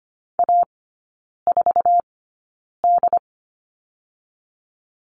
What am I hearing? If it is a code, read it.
Morse code, heard as a keyed tone, about 25 wpm, spaced as for 9 wpm, 690 Hz. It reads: A4D
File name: Lesson 14e - Number 2 - Three Characters 25wpm